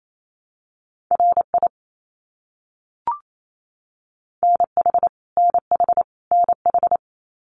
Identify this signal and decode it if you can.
Morse code, heard as a keyed tone, about 28 wpm, 700 Hz. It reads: RI NH NH NH
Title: US States Repeated 3x 28wpm